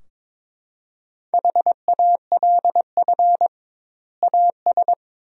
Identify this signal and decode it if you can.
Morse code, heard as a keyed tone, about 22 wpm, 700 Hz. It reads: HALF AS
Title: 2x Word-Spacing - Sentences from Top 400 Words - 22wpm